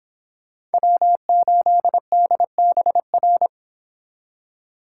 Morse code, heard as a keyed tone, about 31 wpm, 700 Hz. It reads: W8DBR